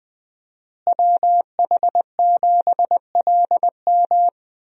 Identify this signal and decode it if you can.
Morse code, heard as a keyed tone, about 21 wpm, 700 Hz. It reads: WH7LM
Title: USA Call Signs - 5 Characters - Repeated 3x - 20wpm